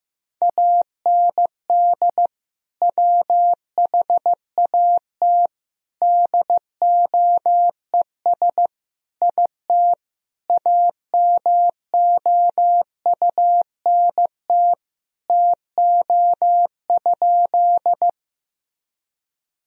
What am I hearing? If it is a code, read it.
Morse code, heard as a keyed tone, about 15 wpm, 700 Hz. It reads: AND WHAT DOES IT AMOUNT TO?
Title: Sentences from Top 1000 Words - Review Part 4 - 15wpm